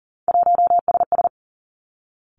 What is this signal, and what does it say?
Morse code, heard as a keyed tone, about 41 wpm, 700 Hz. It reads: 1SS
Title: Three Challenging Letter-Number Combinations 40wpm